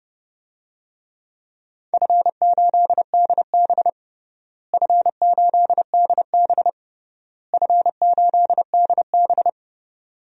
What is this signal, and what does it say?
Morse code, heard as a keyed tone, about 32 wpm, 700 Hz. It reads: F8DB F8DB F8DB
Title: DXCC-100 Call Signs - 4 Characters - Repeated 3x - 30wpm